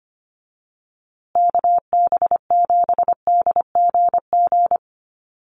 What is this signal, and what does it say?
Morse code, heard as a keyed tone, about 25 wpm, 700 Hz. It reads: KB7DGG
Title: USA Call Signs Repeated 2x 25wpm